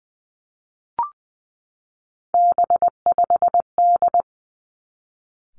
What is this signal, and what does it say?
Morse code, heard as a keyed tone, about 20 wpm, 700 Hz. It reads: B5D
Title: Three Challenging Letter-Number Combinations 20wpm